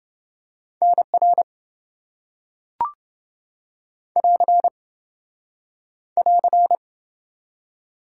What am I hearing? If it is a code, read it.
Morse code, heard as a keyed tone, about 26 wpm, 700 Hz. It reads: NR <AR> <AR>